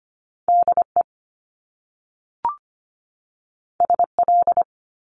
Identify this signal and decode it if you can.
Morse code, heard as a keyed tone, about 25 wpm, 700 Hz. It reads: D E S L
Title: Top Two Letter Combinations in Words 25wpm